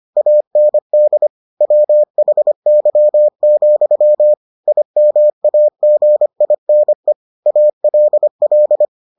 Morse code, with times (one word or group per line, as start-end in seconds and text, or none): 0.2-1.3 AND
1.6-4.3 WHY,
4.7-7.1 IMAGINE
7.5-8.8 ALL